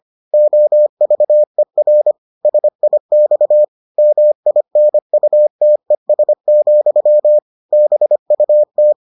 Morse code, heard as a keyed tone, about 25 wpm, 600 Hz. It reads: OVER SIX MINUTES, BUT